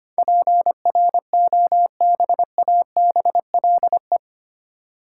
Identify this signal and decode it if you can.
Morse code, heard as a keyed tone, about 25 wpm, 700 Hz. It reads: PROBABLE
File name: Eight Letter Words - 25wpm